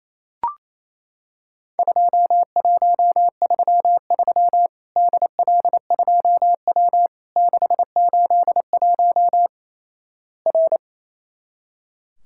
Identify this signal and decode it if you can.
Morse code, heard as a keyed tone, about 28 wpm, 700 Hz. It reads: 2133 DL2W 681 R
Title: Worked All Europe DX Contest - QTC Exchange - 28wpm